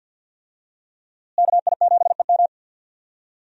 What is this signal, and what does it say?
Morse code, heard as a keyed tone, about 46 wpm, 700 Hz. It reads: KI7EM
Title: USA Call Signs - 5 Characters - Repeated 2x - 50wpm